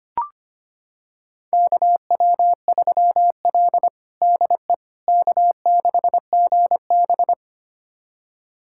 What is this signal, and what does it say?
Morse code, heard as a keyed tone, about 25 wpm, 700 Hz. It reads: KW3L DE K6GB
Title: USA Callsign DE Callsign - 4 Characters - 25wpm